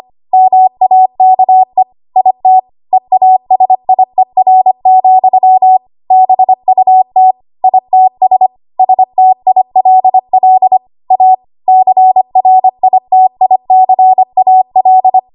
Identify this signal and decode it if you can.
Morse code, heard as a keyed tone, about 25 wpm, 750 Hz. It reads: MAKE IT EASIER, BUT ITS STILL A CRITICAL
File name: Archive / 25 WPM / 25WPM